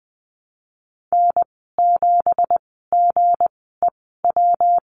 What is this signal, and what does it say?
Morse code, heard as a keyed tone, about 20 wpm, spaced as for 15 wpm, 700 Hz. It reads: N7GEW
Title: 179 Lesson 29g - Slash - Call Signs 20wpm